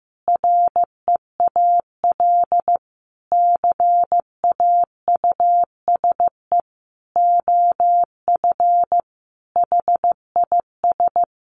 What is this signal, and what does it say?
Morse code, heard as a keyed tone, about 15 wpm, 700 Hz. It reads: REAL CAUSE OF HIS
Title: Sentences from Top 1100 Words 15wpm